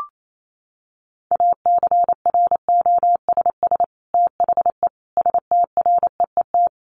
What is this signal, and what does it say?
Morse code, heard as a keyed tone, about 28 wpm, 700 Hz. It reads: ACROSS THE STREET
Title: Sets of 5 Words Encore 28wpm